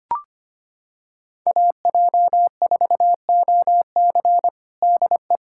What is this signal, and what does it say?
Morse code, heard as a keyed tone, about 25 wpm, 700 Hz. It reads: AJ4OC DE
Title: Practice QSO - Exchange 4 - Ending - 25wpm